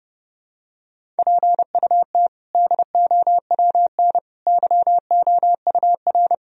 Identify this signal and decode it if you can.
Morse code, heard as a keyed tone, about 32 wpm, 700 Hz. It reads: PUT DOWN YOUR
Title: Sentences from Top 1000 Words - Review Part 1 - 30wpm